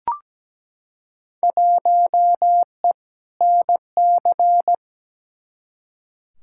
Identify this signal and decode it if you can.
Morse code, heard as a keyed tone, about 17 wpm, 700 Hz. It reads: 1E NC
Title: Field Day Exchanges - 17wpm